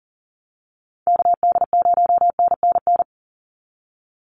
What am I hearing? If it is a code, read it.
Morse code, heard as a keyed tone, about 40 wpm, 700 Hz. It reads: KD0NNN
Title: USA Call Signs 40wpm